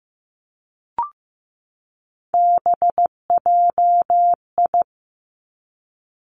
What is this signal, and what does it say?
Morse code, heard as a keyed tone, about 15 wpm, 700 Hz. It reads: BJI
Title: Three Challenging Letter-Number Combinations 15wpm